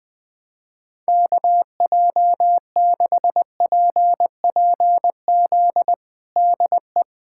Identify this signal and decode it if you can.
Morse code, heard as a keyed tone, about 20 wpm, 700 Hz. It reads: KJ6PPZ DE
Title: USA Callsign DE Callsign 20wpm